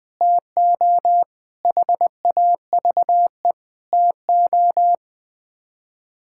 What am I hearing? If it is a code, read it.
Morse code, heard as a keyed tone, about 20 wpm, 700 Hz. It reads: TO HAVE TO
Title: Build A Phrase - 4-Word Phrases - 20wpm